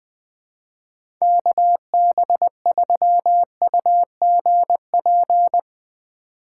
Morse code, heard as a keyed tone, about 20 wpm, 700 Hz. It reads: KB3UGP
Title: USA Call Signs - 2x Character Spacing - 20wpm